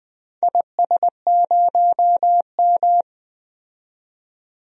Morse code, heard as a keyed tone, about 20 wpm, 700 Hz. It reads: IS0M
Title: DXCC-100 Call Signs - 4 Characters - 20wpm